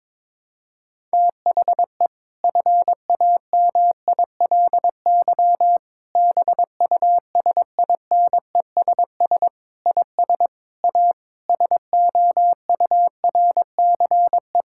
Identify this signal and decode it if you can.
Morse code, heard as a keyed tone, about 22 wpm, 700 Hz. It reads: THE FAMILY BUSINESS IS A SOURCE
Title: Sentences from Top 1000 Words - Review Part 6 - 22wpm